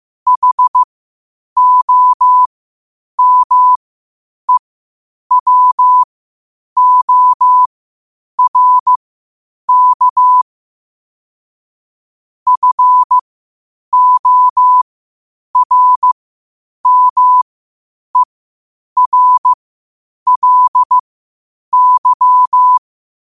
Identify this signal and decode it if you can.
Morse code, heard as a keyed tone, about 15 wpm, spaced as for 8 wpm, 1000 Hz. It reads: HOMEWORK FORMERLY
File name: qotdfiles / 8wpm